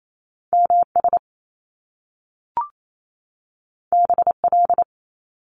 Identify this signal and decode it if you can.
Morse code, heard as a keyed tone, about 28 wpm, 700 Hz. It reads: M S B L
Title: Top Two Letter Combinations in Words 28wpm